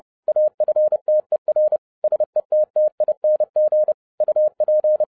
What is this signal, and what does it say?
Morse code, heard as a keyed tone, about 30 wpm, 610 Hz. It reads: AFTER SETTING UP